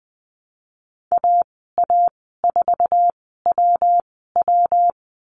Morse code, heard as a keyed tone, about 20 wpm, spaced as for 15 wpm, 700 Hz. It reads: AA4WW
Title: 122 Lesson 20g - Number 3 - Call Signs 20wpm